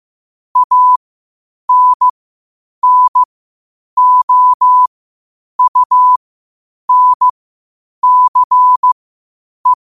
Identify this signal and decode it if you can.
Morse code, heard as a keyed tone, about 15 wpm, spaced as for 8 wpm, 1000 Hz. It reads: ANNOUNCE
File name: qotdfiles / 8wpm